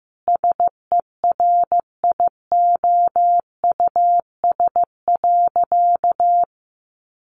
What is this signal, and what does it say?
Morse code, heard as a keyed tone, about 15 wpm, 700 Hz. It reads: SERIOUS.
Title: Sentences from Top 1100 Words 15wpm